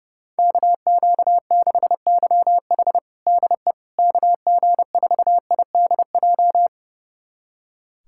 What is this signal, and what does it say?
Morse code, heard as a keyed tone, about 32 wpm, 700 Hz. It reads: KQ6YH DE KG4IDJ
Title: USA Callsign DE Callsign 30wpm